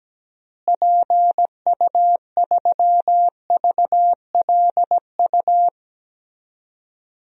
Morse code, heard as a keyed tone, about 17 wpm, 700 Hz. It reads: PU3VLU
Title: DXCC-100 Call Signs - 6 Characters - Repeated 2x - 17wpm